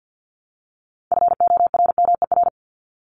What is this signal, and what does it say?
Morse code, heard as a keyed tone, about 49 wpm, 700 Hz. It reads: FORMER
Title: Top 1751-2000 Words 50wpm